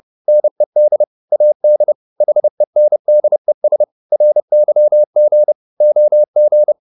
Morse code, heard as a keyed tone, about 30 wpm, 600 Hz. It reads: NED AD HENDES RYG OG
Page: Ferie 30 wpm